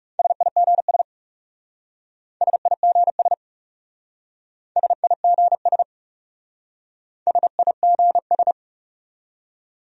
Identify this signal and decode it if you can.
Morse code, heard as a keyed tone, about 38 wpm, 700 Hz. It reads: SIGS SIGS SIGS SIGS